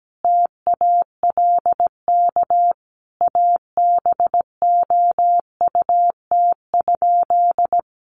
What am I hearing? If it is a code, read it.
Morse code, heard as a keyed tone, about 17 wpm, 700 Hz. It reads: TALK ABOUT?
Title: Sentences from Top 400 Words 17wpm